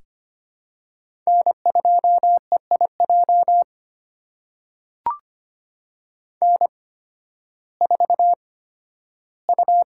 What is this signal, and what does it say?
Morse code, heard as a keyed tone, about 25 wpm, spaced as for 13 wpm, 700 Hz. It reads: N2EIJ N4U
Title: USA Call Signs - 5 Characters - 8x Character Spacing - 25wpm